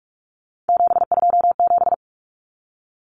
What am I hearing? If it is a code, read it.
Morse code, heard as a keyed tone, about 44 wpm, 700 Hz. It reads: 727
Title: Numbers - 3-Digit 45wpm